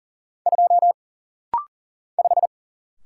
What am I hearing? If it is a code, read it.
Morse code, heard as a keyed tone, about 40 wpm, 700 Hz. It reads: 2 5
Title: Single Letter-Number - Rapid-Fire 40wpm